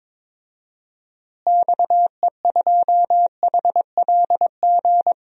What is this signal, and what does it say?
Morse code, heard as a keyed tone, about 22 wpm, 700 Hz. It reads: XE2HLG
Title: DXCC-100 Call Signs - 6 Characters - Repeated 2x - 22wpm